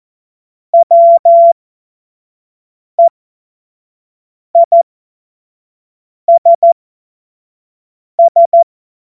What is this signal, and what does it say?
Morse code, heard as a keyed tone, about 14 wpm, spaced as for 5 wpm, 680 Hz. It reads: WEISS
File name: Morsekurs / Witze / Witz 27 Februar 2024 25 F70